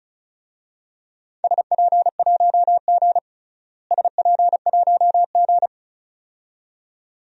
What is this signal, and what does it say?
Morse code, heard as a keyed tone, about 36 wpm, 700 Hz. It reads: SP1G SP1G
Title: DXCC-100 Call Signs - 4 Characters - Repeated 2x - 35wpm